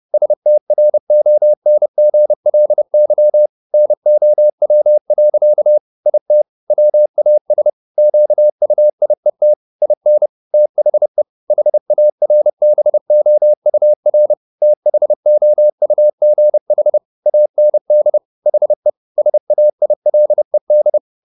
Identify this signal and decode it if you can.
Morse code, heard as a keyed tone, about 30 wpm, 600 Hz. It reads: STRONGLY NOW. IT WAS QUIET IN THE HARBOUR THOUGH AND HE SAILED